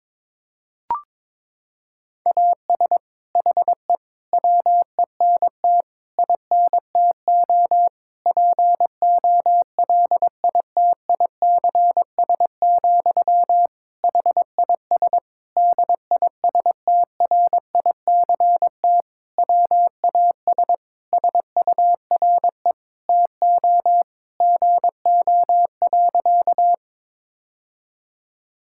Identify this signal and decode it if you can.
Morse code, heard as a keyed tone, about 22 wpm, 700 Hz. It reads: AS HE WENT INTO POLITICS, HIS DISTRICT WAS SURE TO GO.
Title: Sentences from Top 1600 Words 22wpm